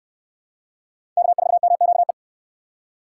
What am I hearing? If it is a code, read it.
Morse code, heard as a keyed tone, about 69 wpm, 700 Hz. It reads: K4MQE